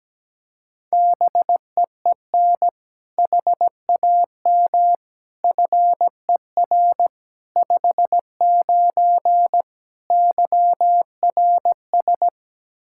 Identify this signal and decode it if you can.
Morse code, heard as a keyed tone, about 17 wpm, 700 Hz. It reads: BEEN HAM FER 59 YRS
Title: Practice QSO - Exchange 3 - More Info - 17wpm